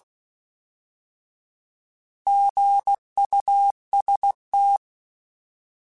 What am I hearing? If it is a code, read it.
Morse code, heard as a keyed tone, about 16 wpm, 800 Hz. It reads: GUST
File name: Alphabetical / Alphabetical mp3s / Gs @15